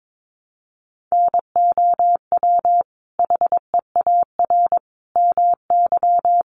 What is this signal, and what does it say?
Morse code, heard as a keyed tone, about 22 wpm, 700 Hz. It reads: NOW HEAR MY
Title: Sentences from Top 1100 Words 22wpm